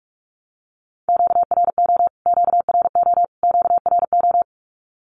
Morse code, heard as a keyed tone, about 44 wpm, 700 Hz. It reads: QRO QRO QRO
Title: Q-Codes Repeated 3x 45wpm